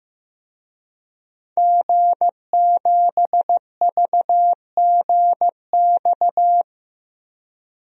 Morse code, heard as a keyed tone, about 15 wpm, 700 Hz. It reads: G7VGX
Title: DXCC-100 Call Signs - 5 Characters - 15wpm